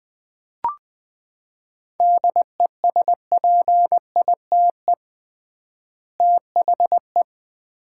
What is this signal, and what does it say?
Morse code, heard as a keyed tone, about 21 wpm, 700 Hz. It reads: DESPITE THE
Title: 3x Word-Spacing - Sets of 4 Words - 20wpm